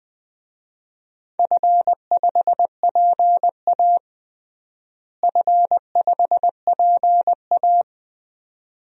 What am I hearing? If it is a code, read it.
Morse code, heard as a keyed tone, about 21 wpm, 700 Hz. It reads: F5PA F5PA
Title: DXCC-100 Call Signs - 4 Characters - Repeated 2x - 20wpm